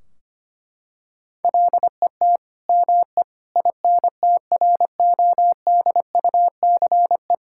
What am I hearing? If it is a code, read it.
Morse code, heard as a keyed tone, about 25 wpm, 700 Hz. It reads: LET ME INTRODUCE
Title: Sentences from Top 1600 Words 25wpm